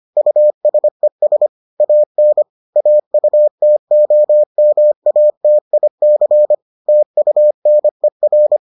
Morse code, heard as a keyed tone, about 25 wpm, 600 Hz. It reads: USES AN AUTOMATIC TUNER